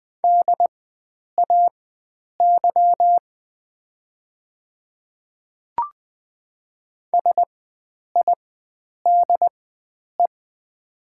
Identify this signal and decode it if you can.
Morse code, heard as a keyed tone, about 20 wpm, spaced as for 9 wpm, 700 Hz. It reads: DAY SIDE
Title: 4x Character Spacing - Top 500 Words - 20wpm